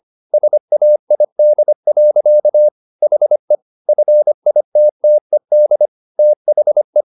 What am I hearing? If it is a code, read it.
Morse code, heard as a keyed tone, about 25 wpm, 600 Hz. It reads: SAID. HE FITTED THE
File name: the old man and the sea part 1 25wpm